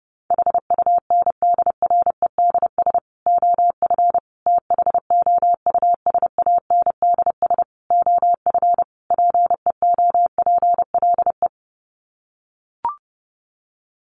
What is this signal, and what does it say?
Morse code, heard as a keyed tone, about 30 wpm, 700 Hz. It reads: HUNDREDS OF THOUSANDS OF PEOPLE
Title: Sets of 5 Words Encore2 30wpm